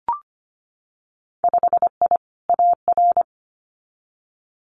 Morse code, heard as a keyed tone, about 25 wpm, 700 Hz. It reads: 5I AR
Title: Winter Field Day Exchanges Encore 25wpm